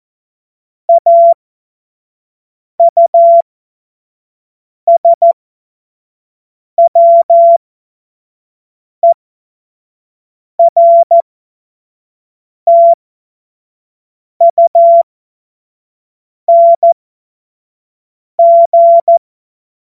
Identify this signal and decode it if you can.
Morse code, heard as a keyed tone, about 14 wpm, spaced as for 5 wpm, 680 Hz. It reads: AUSWERTUNG